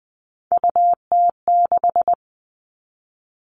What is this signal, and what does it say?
Morse code, heard as a keyed tone, about 20 wpm, 700 Hz. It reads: UT6
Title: All Cut Numbers - 3-Digit 20wpm